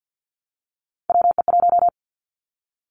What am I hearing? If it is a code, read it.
Morse code, heard as a keyed tone, about 50 wpm, 700 Hz. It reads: WE1
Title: Three Challenging Letter-Number Combinations 50wpm